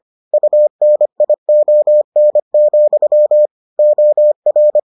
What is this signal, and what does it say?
Morse code, heard as a keyed tone, about 25 wpm, 600 Hz. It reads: UNION, OR